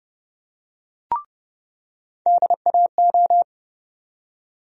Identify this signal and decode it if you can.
Morse code, heard as a keyed tone, about 30 wpm, 700 Hz. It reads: DAO
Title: USA Call Sign Postfixes - 3 Letters 30wpm